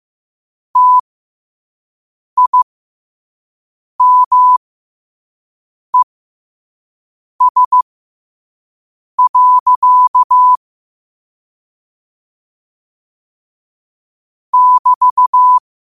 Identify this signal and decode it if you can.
Morse code, heard as a keyed tone, about 15 wpm, spaced as for 5 wpm, 1000 Hz. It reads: TIMES. <BT>
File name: qotdfiles / 5wpm